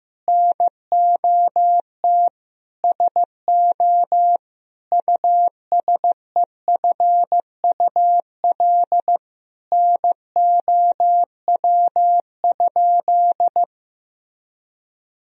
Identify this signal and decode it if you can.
Morse code, heard as a keyed tone, about 15 wpm, 700 Hz. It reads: NOT SO USEFUL NOW?
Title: Sentences from Top 1000 Words - Review Part 4 - 15wpm